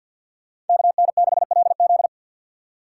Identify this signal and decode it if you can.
Morse code, heard as a keyed tone, about 51 wpm, 700 Hz. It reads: KN6LZ
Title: USA Call Signs - 5 Characters - 2x Character Spacing - 50wpm